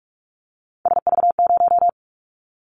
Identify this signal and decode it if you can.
Morse code, heard as a keyed tone, about 46 wpm, 700 Hz. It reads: SV0
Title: Callsign Prefixes for Top 50 DXCC - 45wpm